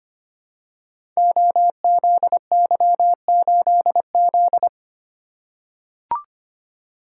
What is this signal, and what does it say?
Morse code, heard as a keyed tone, about 25 wpm, 700 Hz. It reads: OZY8Z
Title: Code Groups - 5 Letter:Numbers - 25wpm